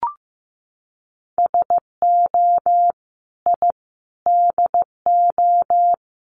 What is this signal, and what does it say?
Morse code, heard as a keyed tone, about 15 wpm, 700 Hz. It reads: SO I DO
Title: Sets of 5 Words 15wpm